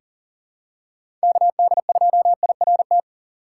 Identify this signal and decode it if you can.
Morse code, heard as a keyed tone, about 40 wpm, 700 Hz. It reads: KD2IRT
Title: USA Call Signs - 6 Characters - 4x Character Spacing - 40wpm